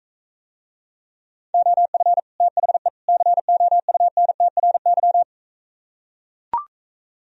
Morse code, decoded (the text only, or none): OF THE COUNTRY